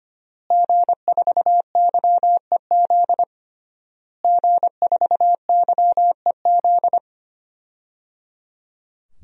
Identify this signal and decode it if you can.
Morse code, heard as a keyed tone, about 25 wpm, 700 Hz. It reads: G4YEZ G4YEZ
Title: DXCC-100 Call Signs - 5 Characters - Repeated 3x - 25wpm